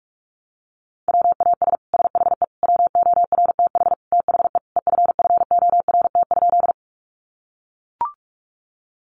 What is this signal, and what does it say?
Morse code, heard as a keyed tone, about 44 wpm, 700 Hz. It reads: WAS SHE WORTH THE EFFORT?
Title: Sentences from Top 1200 Words 45wpm